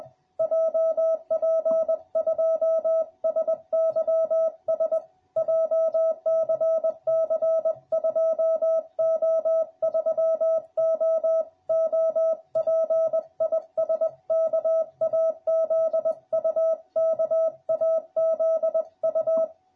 Morse code, heard as a keyed tone, about 20 wpm, 650 Hz. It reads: JP2SYS JCC 2O3O OP IS KAZU KAZU